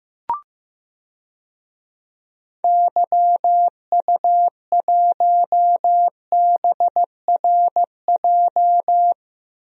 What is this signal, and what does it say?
Morse code, heard as a keyed tone, about 15 wpm, 700 Hz. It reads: YU1BRJ
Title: DXCC-100 Call Signs - 6 Characters - Repeated 2x - 15wpm